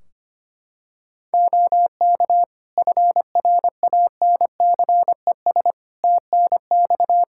Morse code, heard as a keyed tone, about 25 wpm, 700 Hz. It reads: OK FRANCES TNX
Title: Practice QSO - Exchange 4 - Ending - 25wpm